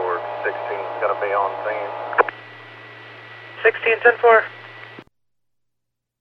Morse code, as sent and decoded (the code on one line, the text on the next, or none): none